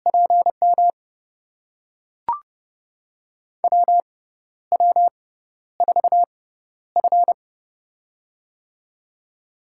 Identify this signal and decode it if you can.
Morse code, heard as a keyed tone, about 29 wpm, spaced as for 12 wpm, 700 Hz. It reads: PM WW4F
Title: USA Call Signs - 4 Characters - 6x Character Spacing - 30wpm